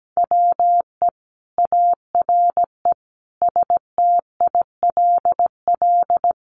Morse code, heard as a keyed tone, about 17 wpm, 700 Hz. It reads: WE ARE STILL
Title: Sentences from Top 1300 Words 17wpm